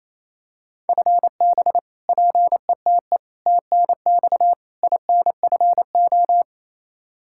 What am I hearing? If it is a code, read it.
Morse code, heard as a keyed tone, about 28 wpm, 700 Hz. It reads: FB PETE TNX INFO
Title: Practice QSO - Exchange 3 - More Info - 28wpm